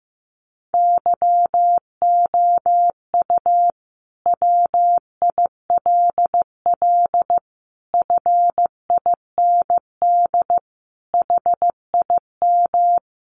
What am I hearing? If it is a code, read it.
Morse code, heard as a keyed tone, about 14 wpm, 700 Hz. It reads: YOU WILL FIND HIM
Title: Sentences from Top 1100 Words 15wpm